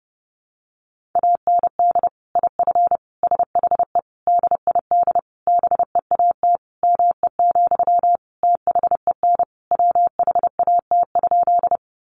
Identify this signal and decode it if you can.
Morse code, heard as a keyed tone, about 30 wpm, 700 Hz. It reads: AND IF SHE DID BEAT ME, THEN WHAT?
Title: Sentences from Top 1300 Words 30wpm